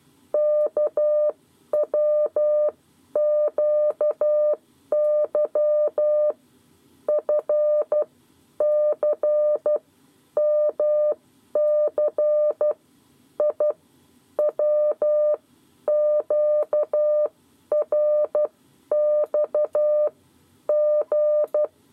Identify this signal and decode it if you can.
Morse code, heard as a keyed tone, about 12 wpm, spaced as for 10 wpm, 600 Hz. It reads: KWQY FC MC I WQRX G
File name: 7 Lezione cw 2